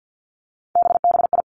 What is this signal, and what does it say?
Morse code, heard as a keyed tone, about 50 wpm, 700 Hz. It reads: BBI
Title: Three Challenging Letter-Number Combinations 50wpm